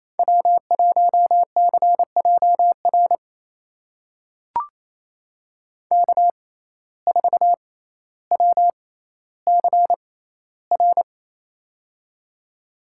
Morse code, heard as a keyed tone, about 28 wpm, spaced as for 14 wpm, 700 Hz. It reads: W1CJR K4WCR
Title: USA Call Signs - 5 Characters - 6x Character Spacing - 28wpm